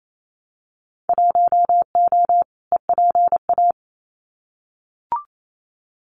Morse code, heard as a keyed tone, about 28 wpm, 700 Hz. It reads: 1O EPA
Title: Winter Field Day Exchanges 28wpm